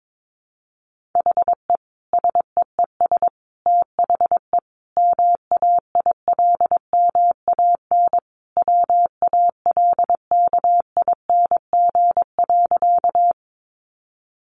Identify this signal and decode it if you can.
Morse code, heard as a keyed tone, about 22 wpm, 700 Hz. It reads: HE SEES THE MAILMAN WALKING.
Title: Additional Easy Phrases 22wpm